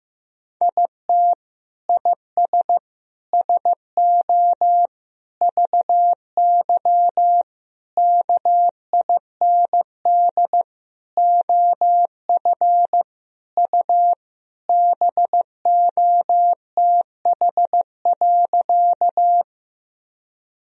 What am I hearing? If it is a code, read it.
Morse code, heard as a keyed tone, about 15 wpm, 700 Hz. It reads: IT IS SO VY KIND OF U BOTH.
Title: Sentences from Top 300 Words and Abbreviations 15wpm